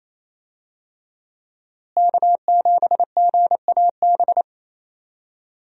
Morse code, heard as a keyed tone, about 28 wpm, 700 Hz. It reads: K7GAB
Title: USA Call Signs - 5 Characters - Repeated 2x - 28wpm